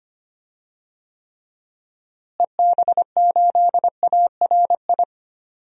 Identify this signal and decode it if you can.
Morse code, heard as a keyed tone, about 25 wpm, 700 Hz. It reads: EB8ARI
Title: DXCC-100 Call Signs - 6 Characters - Repeated 3x - 25wpm